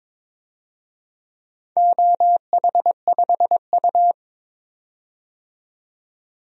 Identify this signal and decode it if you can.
Morse code, heard as a keyed tone, about 22 wpm, 700 Hz. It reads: OH5U
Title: DXCC-100 Call Signs - 4 Characters - 22wpm